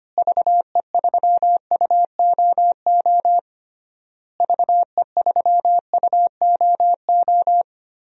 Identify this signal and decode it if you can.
Morse code, heard as a keyed tone, about 25 wpm, 700 Hz. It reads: VE3UOO VE3UOO
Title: DXCC-100 Call Signs - 6 Characters - Repeated 3x - 25wpm